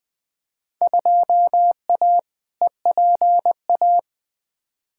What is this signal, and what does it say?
Morse code, heard as a keyed tone, about 20 wpm, 700 Hz. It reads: 2A EPA
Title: Field Day Exchanges - 20wpm